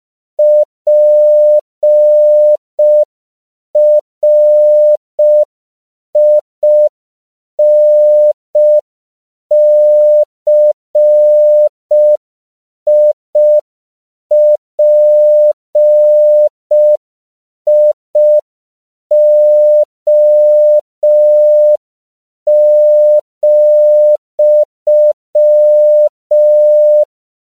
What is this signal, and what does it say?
Morse code, heard as a keyed tone, about 5 wpm, 600 Hz. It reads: PRINCIPIO,
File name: CW / 5PPM